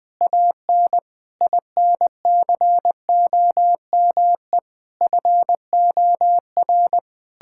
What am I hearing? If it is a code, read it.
Morse code, heard as a keyed tone, about 20 wpm, 700 Hz. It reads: AN INCOME FOR
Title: Sentences from Top 1000 Words - Review Part 5 - 20wpm